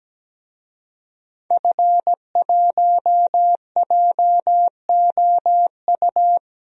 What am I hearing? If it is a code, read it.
Morse code, heard as a keyed tone, about 17 wpm, 700 Hz. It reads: F1JOU
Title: DXCC-100 Call Signs - 5 Characters - 17wpm